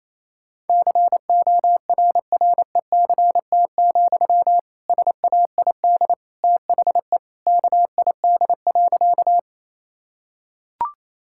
Morse code, heard as a keyed tone, about 28 wpm, 700 Hz. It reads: CORRECT, SAID THE KID.
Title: Sentences from Top 1600 Words 28wpm